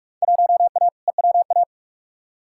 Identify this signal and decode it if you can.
Morse code, heard as a keyed tone, about 46 wpm, 700 Hz. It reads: 1A EWA
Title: Field Day Exchanges Encore - 45wpm